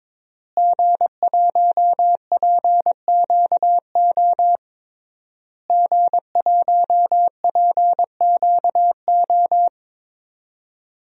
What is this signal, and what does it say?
Morse code, heard as a keyed tone, about 22 wpm, 700 Hz. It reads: G1PQO G1PQO
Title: DXCC-100 Call Signs - 5 Characters - Repeated 3x - 22wpm